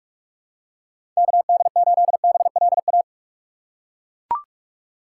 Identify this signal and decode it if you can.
Morse code, heard as a keyed tone, about 42 wpm, 700 Hz. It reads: KD8BLA
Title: USA Call Signs - 2x Character Spacing - 45wpm